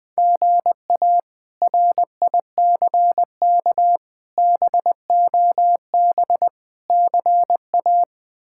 Morse code, heard as a keyed tone, about 19 wpm, 700 Hz. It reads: GA RICK BOB CA
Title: K1USN SST QSO Practice - 20wpm